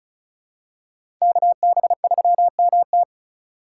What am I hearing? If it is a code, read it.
Morse code, heard as a keyed tone, about 36 wpm, 700 Hz. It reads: KB3MT
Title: USA Call Signs - 8x Character Spacing - 35wpm